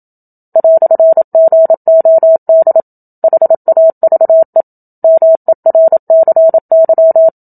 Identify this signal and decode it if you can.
Morse code, heard as a keyed tone, about 27 wpm, 650 Hz. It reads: "GOD HAVE MERCY
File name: Book Two / 01-Chapter 1 - 27 WPM